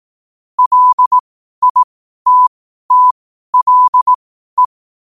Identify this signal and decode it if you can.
Morse code, heard as a keyed tone, about 18 wpm, spaced as for 12 wpm, 1000 Hz. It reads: LITTLE
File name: qotdfiles / 12wpm